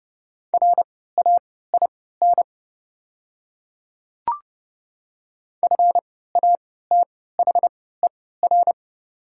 Morse code, heard as a keyed tone, about 30 wpm, spaced as for 17 wpm, 700 Hz. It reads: RAIN FATHER
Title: 3x Character Spacing - Top 300 Words - 30wpm